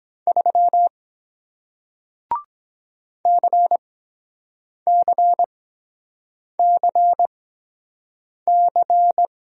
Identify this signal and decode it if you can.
Morse code, heard as a keyed tone, about 23 wpm, spaced as for 7 wpm, 700 Hz. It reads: 3 CCCC